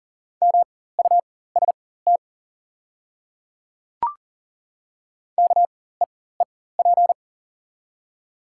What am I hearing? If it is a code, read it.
Morse code, heard as a keyed tone, about 40 wpm, spaced as for 19 wpm, 700 Hz. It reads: MUST KEEP
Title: 4x Character Spacing - Top 300 Words - 40wpm